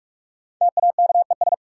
Morse code, heard as a keyed tone, about 42 wpm, 700 Hz. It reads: TAKES
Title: Sentences from Top 1000 Words - Review Part 6 - New Word Drill - 45wpm